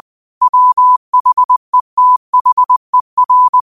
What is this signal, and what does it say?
Morse code, heard as a keyed tone, about 20 wpm, 1000 Hz. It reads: WHETHER
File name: qotdfiles / 20wpm